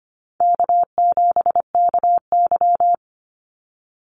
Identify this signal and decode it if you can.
Morse code, heard as a keyed tone, about 25 wpm, 700 Hz. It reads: K7KY
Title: Easy Call Signs Repeated 2x - 25wpm